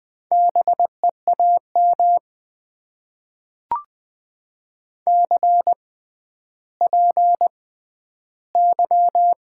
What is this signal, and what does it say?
Morse code, heard as a keyed tone, about 20 wpm, spaced as for 11 wpm, 700 Hz. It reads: BEAM CPY
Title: Top 90 QSO Elements - 6x Character Spacing - 20wpm